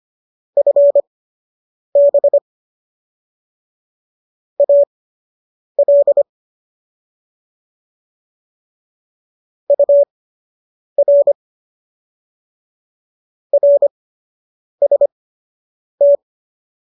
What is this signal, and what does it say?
Morse code, heard as a keyed tone, about 25 wpm, spaced as for 8 wpm, 570 Hz. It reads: FB AL UR RST